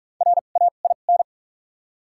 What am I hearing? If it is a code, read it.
Morse code, heard as a keyed tone, about 45 wpm, spaced as for 32 wpm, 700 Hz. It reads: RAIN
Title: 2x Character Spacing - Top 500 Words - 45wpm